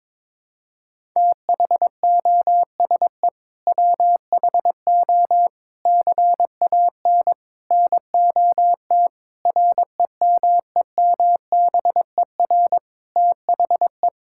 Morse code, heard as a keyed tone, about 23 wpm, 700 Hz. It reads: THOSE WHO CAN NOT REMEMBER THE
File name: practice / 22wpm / Sets of 8 Words - 22wpm